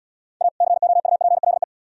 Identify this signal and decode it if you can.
Morse code, heard as a keyed tone, about 68 wpm, 700 Hz. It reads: A COUPLE